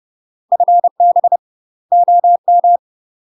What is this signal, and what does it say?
Morse code, heard as a keyed tone, about 30 wpm, 700 Hz. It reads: FB OM
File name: UK MASTER QSOs TRAINING / 30WPM / UK MASTER QSOs 0011